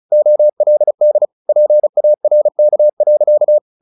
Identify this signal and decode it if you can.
Morse code, heard as a keyed tone, about 34 wpm, 600 Hz. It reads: OLD PARK.
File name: the old man and the sea / the old man and the sea 1 35wpm